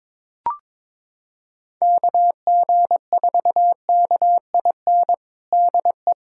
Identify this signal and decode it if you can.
Morse code, heard as a keyed tone, about 22 wpm, 700 Hz. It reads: KG4KIN DE
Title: USA Callsign DE Callsign 22wpm